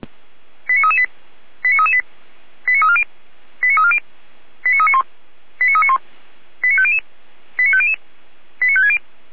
Five-tone selective call (ZVEI), heard as quick runs of five tones, tones of about 70 ms. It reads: 89338 89338 89340 89340 89391 89391 89500 89500 89560